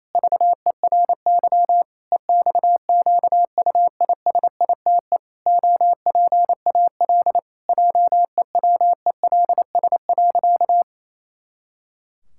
Morse code, heard as a keyed tone, about 28 wpm, 700 Hz. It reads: VERY EXQUISITE OPAL JEWELS.
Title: Pangram Challenge v2 - 28wpm